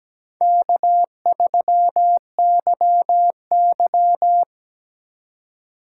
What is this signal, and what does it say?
Morse code, heard as a keyed tone, about 17 wpm, 700 Hz. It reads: K3YY
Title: USA Call Signs - 4 Characters - 4x Character Spacing - 17wpm